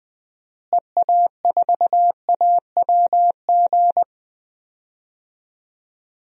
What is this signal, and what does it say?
Morse code, heard as a keyed tone, about 20 wpm, 700 Hz. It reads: EA4AWG
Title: DXCC-100 Call Signs - 6 Characters - Repeated 3x - 20wpm